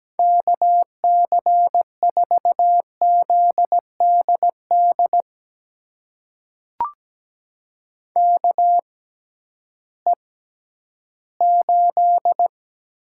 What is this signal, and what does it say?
Morse code, heard as a keyed tone, about 17 wpm, spaced as for 11 wpm, 700 Hz. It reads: KC4ZDD KE8
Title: USA Call Signs - 6 Characters - 6x Character Spacing - 17wpm